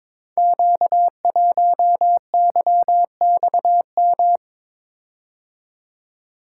Morse code, heard as a keyed tone, about 22 wpm, 700 Hz. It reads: Q1YXM
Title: Code Groups - 5 Letter:Numbers - 22wpm